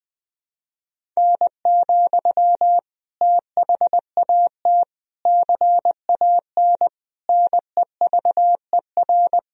Morse code, heard as a keyed tone, about 20 wpm, 700 Hz. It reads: N, THAT CAN NEVER
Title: Sentences from Top 300 Words and Abbreviations 20wpm